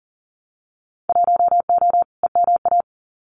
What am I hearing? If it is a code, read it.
Morse code, heard as a keyed tone, about 41 wpm, 700 Hz. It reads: 1O EMA
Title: Winter Field Day Exchanges Encore 40wpm